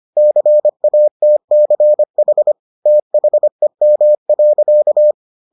Morse code, heard as a keyed tone, about 25 wpm, 600 Hz. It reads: CATCH THEM.